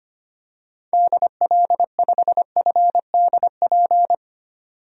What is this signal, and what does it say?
Morse code, heard as a keyed tone, about 25 wpm, 700 Hz. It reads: DL5FDP
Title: DXCC-100 Call Signs - 6 Characters - 25wpm